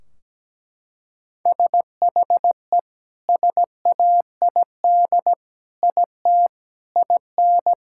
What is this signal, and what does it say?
Morse code, heard as a keyed tone, about 17 wpm, 700 Hz. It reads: SHE SAID IT IN
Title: Sentences from Top 1600 Words 17wpm